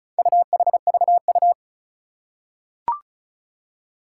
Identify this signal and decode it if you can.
Morse code, heard as a keyed tone, about 36 wpm, 700 Hz. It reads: UHVU
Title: Four Challenging Letter-Number Combinations 35wpm